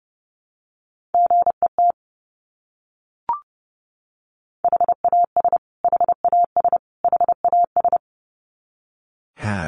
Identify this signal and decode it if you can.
Morse code, heard as a keyed tone, about 28 wpm, 700 Hz. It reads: GET HAS HAS HAS
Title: Top 200 Words Repeated 30wpm